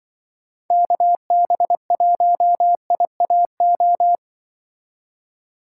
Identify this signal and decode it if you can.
Morse code, heard as a keyed tone, about 24 wpm, 700 Hz. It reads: KB1IAO
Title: Speed Racing - USA 2x3 Call Signs 20wpm